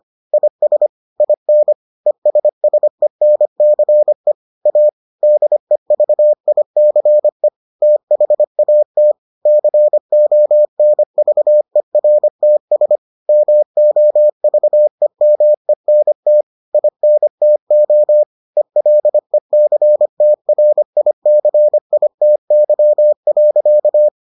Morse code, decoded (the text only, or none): IS IN ESSENCE A DEVICE THAT CONVERTS MOVEMENT INTO ELECTRICITY.